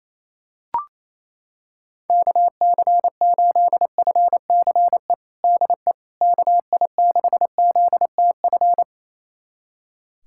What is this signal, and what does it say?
Morse code, heard as a keyed tone, about 28 wpm, 700 Hz. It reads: KC8FCE DE KI6ZTF
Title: USA Callsign DE Callsign - 6 Characters - 28wpm